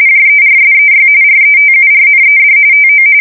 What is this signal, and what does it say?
RTTY, about 45.45 baud. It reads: JERSEY RETURNED TO
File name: fsk170 / battleshipnj